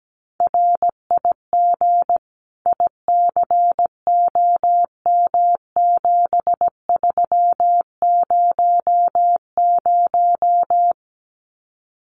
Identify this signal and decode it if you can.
Morse code, heard as a keyed tone, about 17 wpm, 700 Hz. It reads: RIG ICOM7300
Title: Common QSO Phrases 17wpm